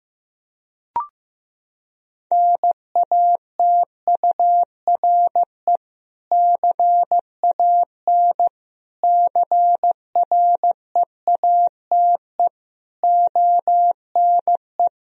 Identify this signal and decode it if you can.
Morse code, heard as a keyed tone, about 16 wpm, 700 Hz. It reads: NATURE CAN CREATE ONE
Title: Sentences from Top 1000 Words - Review Part 3 - 15wpm